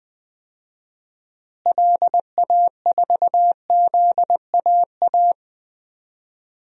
Morse code, heard as a keyed tone, about 20 wpm, 700 Hz. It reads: LA4ZAA
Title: DXCC-100 Call Signs - 6 Characters - Repeated 2x - 20wpm